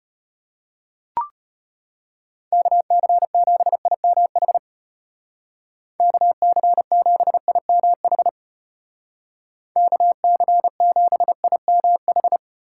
Morse code, heard as a keyed tone, about 34 wpm, 700 Hz. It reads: KC7IMH KC7IMH KC7IMH